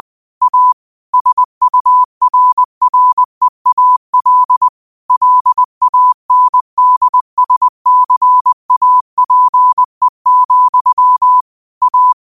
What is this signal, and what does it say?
Morse code, heard as a keyed tone, about 20 wpm, 1000 Hz. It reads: A SURREAL LANDSCAPE, A
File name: qotdfiles / 20wpm